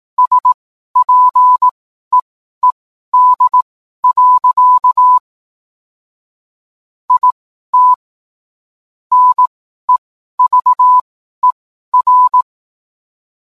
Morse code, heard as a keyed tone, about 18 wpm, spaced as for 12 wpm, 1000 Hz. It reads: SPEED. IT NEVER
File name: cwblog / cw mp3 directory / mem morse 12